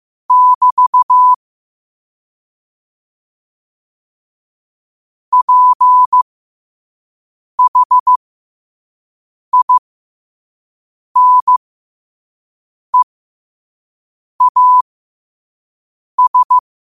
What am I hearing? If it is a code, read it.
Morse code, heard as a keyed tone, about 15 wpm, spaced as for 5 wpm, 1000 Hz. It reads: <BT> PHINEAS
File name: qotdfiles / 5wpm